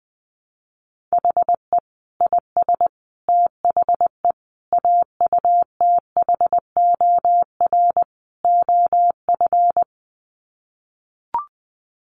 Morse code, heard as a keyed tone, about 20 wpm, 700 Hz. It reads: HE IS THE AUTHOR OF
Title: Sets of 5 Words Encore2 20wpm